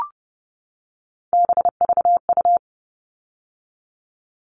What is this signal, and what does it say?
Morse code, heard as a keyed tone, about 30 wpm, 700 Hz. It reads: BVU
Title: All Cut Numbers - 3-Digit 30wpm